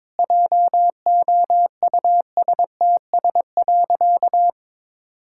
Morse code, heard as a keyed tone, about 22 wpm, 700 Hz. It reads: JOUSTS.
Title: Pangram Challenge v2 - 22wpm